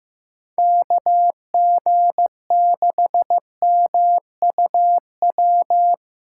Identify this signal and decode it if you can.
Morse code, heard as a keyed tone, about 15 wpm, 700 Hz. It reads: KG6MUW
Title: USA Callsign DE Callsign 15wpm